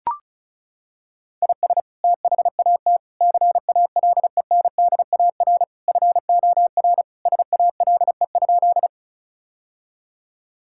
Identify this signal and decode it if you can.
Morse code, heard as a keyed tone, about 34 wpm, 700 Hz. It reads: IS THAT CALENDAR FOR SALE?
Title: Sentences from Top 1600 Words 35wpm